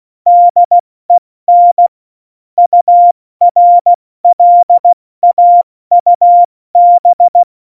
Morse code, heard as a keyed tone, about 16 wpm, 700 Hz. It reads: DEN URLAUB
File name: Morsekurs / Witze / Witz 18. Juni 2019, 70 F80